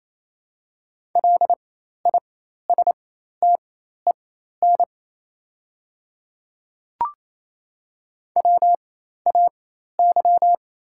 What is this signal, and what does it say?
Morse code, heard as a keyed tone, about 28 wpm, spaced as for 13 wpm, 700 Hz. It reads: LISTEN WAY
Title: 4x Character Spacing - Top 500 Words - 28wpm